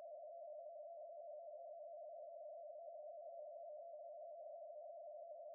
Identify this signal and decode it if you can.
Two-tone teleprinter signal, about 80 Hz shift.